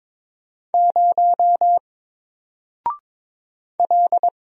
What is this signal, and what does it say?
Morse code, heard as a keyed tone, about 22 wpm, 700 Hz. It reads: 0 L
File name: Single-Letter-Number-22wpm